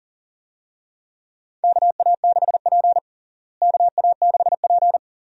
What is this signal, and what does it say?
Morse code, heard as a keyed tone, about 37 wpm, 700 Hz. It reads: KA6P KA6P
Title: USA Call Signs - 4 Characters - Repeated 2x - 40wpm